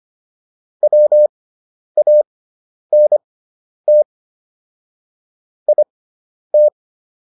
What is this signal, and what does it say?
Morse code, heard as a keyed tone, about 25 wpm, spaced as for 10 wpm, 600 Hz. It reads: WANT IT